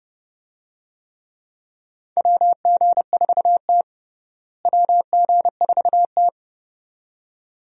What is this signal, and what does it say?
Morse code, heard as a keyed tone, about 30 wpm, 700 Hz. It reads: WG4T WG4T
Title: USA Call Signs - 4 Characters - Repeated 2x - 30wpm